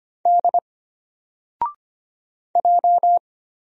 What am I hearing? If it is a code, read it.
Morse code, heard as a keyed tone, about 25 wpm, 700 Hz. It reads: D J
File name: Single-Letter-Number-25wpm